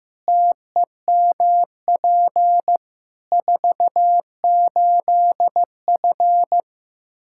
Morse code, heard as a keyed tone, about 15 wpm, 700 Hz. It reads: TEMP 48F
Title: Practice QSO - Exchange 2 - Info - 15wpm